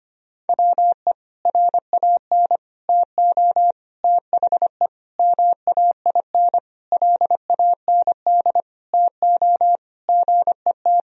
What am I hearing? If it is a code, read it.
Morse code, heard as a keyed tone, about 25 wpm, 700 Hz. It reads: WE RAN TO THE MAIN LAND TO GET
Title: Sentences from Top 1000 Words - Review Part 2 - 25wpm